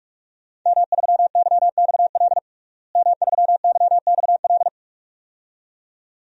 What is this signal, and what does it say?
Morse code, heard as a keyed tone, about 46 wpm, 700 Hz. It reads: M3YXL M3YXL
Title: DXCC-100 Call Signs - 5 Characters - Repeated 2x - 45wpm